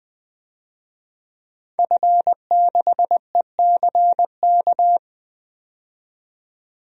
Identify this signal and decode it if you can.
Morse code, heard as a keyed tone, about 20 wpm, 700 Hz. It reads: F6ECK
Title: DXCC-100 Call Signs - 5 Characters - 20wpm